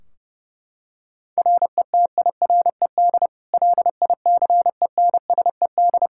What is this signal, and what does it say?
Morse code, heard as a keyed tone, about 30 wpm, 700 Hz. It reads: RETIRED LICENSED
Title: K1USN SST QSO Practice 30wpm